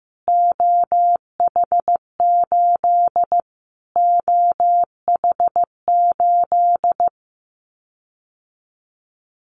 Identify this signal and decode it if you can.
Morse code, heard as a keyed tone, about 15 wpm, 700 Hz. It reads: OH8 OH8
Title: Callsign Prefixes for Top 25 DXCC - Repeated 3x - 15wpm